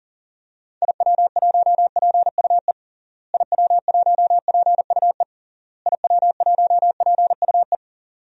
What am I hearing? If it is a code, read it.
Morse code, heard as a keyed tone, about 40 wpm, 700 Hz. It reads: IW1PUE IW1PUE IW1PUE
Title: DXCC-100 Call Signs - 6 Characters - Repeated 3x - 40wpm